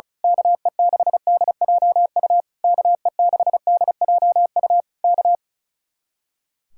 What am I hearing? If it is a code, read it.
Morse code, heard as a keyed tone, about 32 wpm, 700 Hz. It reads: KE6DJU KE6DJU K
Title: Speed-Racing Common QSO Phrases Encore - 35wpm